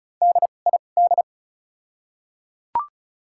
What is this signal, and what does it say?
Morse code, heard as a keyed tone, about 36 wpm, spaced as for 25 wpm, 700 Hz. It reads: DID
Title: 2x Character Spacing - Top 100 Words - 35wpm